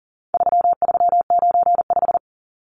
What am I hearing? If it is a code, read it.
Morse code, heard as a keyed tone, about 39 wpm, 700 Hz. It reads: 3395
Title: Numbers - 4-Digit 40wpm